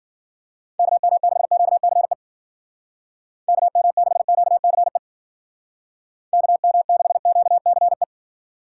Morse code, heard as a keyed tone, about 49 wpm, 700 Hz. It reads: KM6QCE KM6QCE KM6QCE
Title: Speed Racing - USA 2x3 Call Signs 40wpm